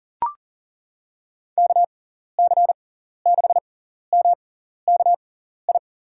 Morse code, heard as a keyed tone, about 40 wpm, spaced as for 14 wpm, 700 Hz. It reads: KC6MKI
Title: USA Call Signs - 6 Characters - 6x Character Spacing - 40wpm